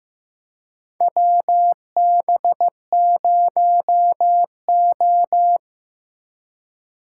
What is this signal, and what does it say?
Morse code, heard as a keyed tone, about 15 wpm, 700 Hz. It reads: WB0O
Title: USA Call Signs - 4 Characters - 4x Character Spacing - 15wpm